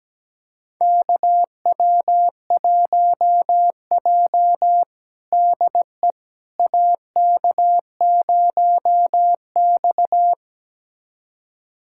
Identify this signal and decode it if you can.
Morse code, heard as a keyed tone, about 17 wpm, 700 Hz. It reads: KW1J DE AK0X
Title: USA Callsign DE Callsign - 4 Characters - 17wpm